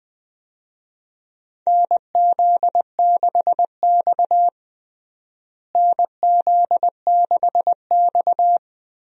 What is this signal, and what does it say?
Morse code, heard as a keyed tone, about 20 wpm, 700 Hz. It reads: NZ6X NZ6X
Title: USA Call Signs - 4 Characters - Repeated 2x - 20wpm